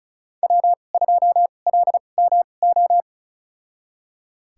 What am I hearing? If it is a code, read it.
Morse code, heard as a keyed tone, about 36 wpm, spaced as for 25 wpm, 700 Hz. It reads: W2LMO
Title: USA Call Signs - 5 Characters - 2x Character Spacing - 35wpm